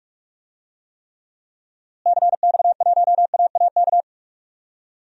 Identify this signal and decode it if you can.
Morse code, heard as a keyed tone, about 46 wpm, 700 Hz. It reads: CX1AAK
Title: DXCC-100 Call Signs - 6 Characters - Repeated 2x - 45wpm